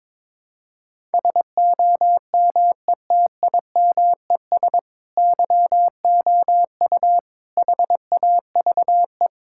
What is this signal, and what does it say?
Morse code, heard as a keyed tone, about 22 wpm, 700 Hz. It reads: SOMETIMES YOU HAVE
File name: Sentences with 7-Word Phrases - 22wpm